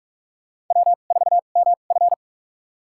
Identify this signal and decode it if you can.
Morse code, heard as a keyed tone, about 42 wpm, spaced as for 31 wpm, 700 Hz. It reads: W4MF
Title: USA Call Signs - 4 Characters - 2x Character Spacing - 45wpm